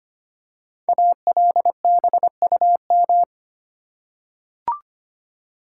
Morse code, heard as a keyed tone, about 25 wpm, 700 Hz. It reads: ALBUM
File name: 5 Letter Words - First 3 Letters Given - 25wpm